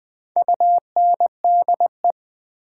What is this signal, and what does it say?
Morse code, heard as a keyed tone, about 20 wpm, 700 Hz. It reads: UNDE
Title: Build A Word - 1 Letter At A Time - 5-Letter Words - 20wpm